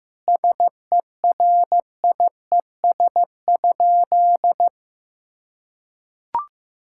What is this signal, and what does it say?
Morse code, heard as a keyed tone, about 15 wpm, 700 Hz. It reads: SERIES?
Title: Sentences from Top 1000 Words - Review Part 5 - 15wpm